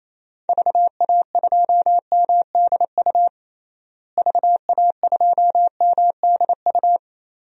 Morse code, heard as a keyed tone, about 28 wpm, 700 Hz. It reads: VA2MDU VA2MDU
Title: DXCC-100 Call Signs - 6 Characters - Repeated 3x - 28wpm